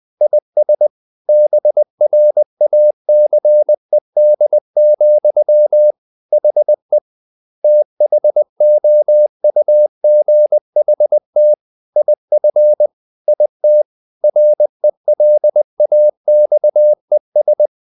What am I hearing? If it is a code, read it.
Morse code, heard as a keyed tone, about 20 wpm, 600 Hz. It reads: IS BRACED, HE THOUGHT IF IT RELAXES
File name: the old man and the sea part 2 20wpm